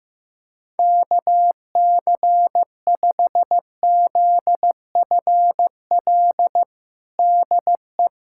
Morse code, heard as a keyed tone, about 15 wpm, 700 Hz. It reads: KC5ZFL DE
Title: USA Callsign DE Callsign - 6 Characters - 15wpm